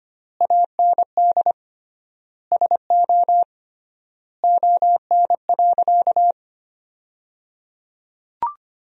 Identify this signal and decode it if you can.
Morse code, heard as a keyed tone, about 25 wpm, 700 Hz. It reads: AND SO ON.
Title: 3x Word-Spacing - Sentences from Top 100 Words - 25wpm